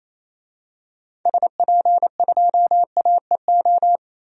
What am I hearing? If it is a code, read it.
Morse code, heard as a keyed tone, about 28 wpm, 700 Hz. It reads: SP2AEO
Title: DXCC-100 Call Signs - 6 Characters - 28wpm